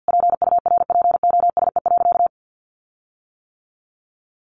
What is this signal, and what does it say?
Morse code, heard as a keyed tone, about 50 wpm, 700 Hz. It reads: PURPOSE.
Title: Sentences from Top 1400 Words 50wpm